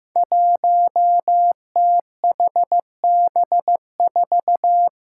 Morse code, heard as a keyed tone, about 15 wpm, 700 Hz. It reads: 1THB4
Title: Five Challenging Letter-Number Combinations 15wpm